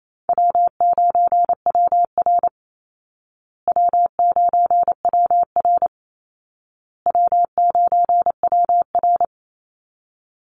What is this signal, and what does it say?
Morse code, heard as a keyed tone, about 28 wpm, 700 Hz. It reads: W9WR W9WR W9WR
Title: Call Signs Repeated 2x - 28wpm